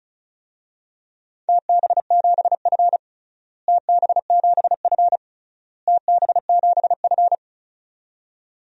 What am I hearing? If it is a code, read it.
Morse code, heard as a keyed tone, about 34 wpm, 700 Hz. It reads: TB7F TB7F TB7F
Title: DXCC-100 Call Signs - 4 Characters - Repeated 3x - 35wpm